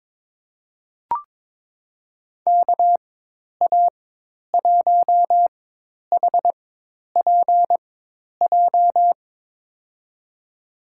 Morse code, heard as a keyed tone, about 22 wpm, spaced as for 10 wpm, 700 Hz. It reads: KA1HPJ
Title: USA Call Signs - 6 Characters - 4x Character Spacing - 22wpm